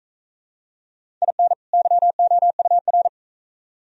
Morse code, heard as a keyed tone, about 44 wpm, 700 Hz. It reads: IN YOUR